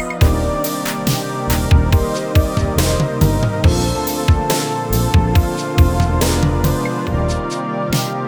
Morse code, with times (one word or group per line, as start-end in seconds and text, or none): none